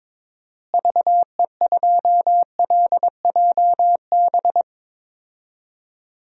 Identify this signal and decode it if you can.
Morse code, heard as a keyed tone, about 22 wpm, 700 Hz. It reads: VE2LJB
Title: DXCC-100 Call Signs - 6 Characters - Repeated 2x - 22wpm